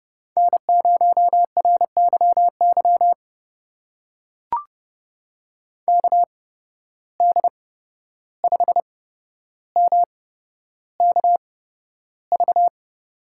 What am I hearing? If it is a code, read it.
Morse code, heard as a keyed tone, about 30 wpm, spaced as for 12 wpm, 700 Hz. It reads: N0RYY KD5MKV
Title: USA Call Signs - 8x Character Spacing - 30wpm